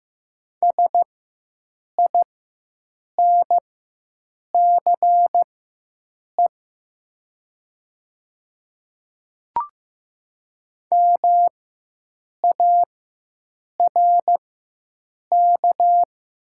Morse code, heard as a keyed tone, about 15 wpm, spaced as for 7 wpm, 700 Hz. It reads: SINCE MARK
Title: 4x Character Spacing - Top 300 Words - 15wpm